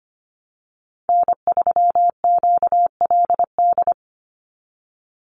Morse code, heard as a keyed tone, about 25 wpm, 700 Hz. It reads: N3QLD
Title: USA Call Signs 25wpm